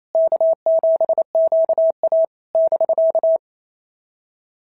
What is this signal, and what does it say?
Morse code, heard as a keyed tone, about 29 wpm, 690 Hz. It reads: K7QA <BK>
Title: SOTA QSO Practice 28wpm